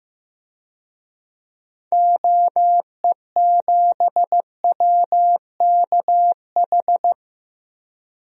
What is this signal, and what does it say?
Morse code, heard as a keyed tone, about 15 wpm, 700 Hz. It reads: OE7WKH
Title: DXCC-100 Call Signs - 6 Characters - 15wpm